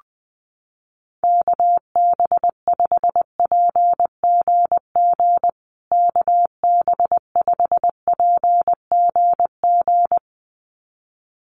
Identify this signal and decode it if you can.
Morse code, heard as a keyed tone, about 20 wpm, 700 Hz. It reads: KB5PGG KB5PGG
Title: USA Call Signs Repeated 2x 20wpm